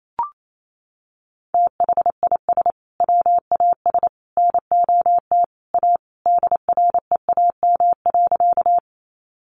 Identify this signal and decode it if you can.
Morse code, heard as a keyed tone, about 28 wpm, 700 Hz. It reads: THIS WAS NOT A DREAM.
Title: Sentences from Top 1500 Words 28wpm